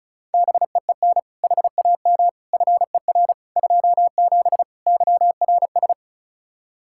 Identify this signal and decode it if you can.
Morse code, heard as a keyed tone, about 36 wpm, 700 Hz. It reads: BEEN HAM FER 27 YRS
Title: K1USN SST QSO Practice 35wpm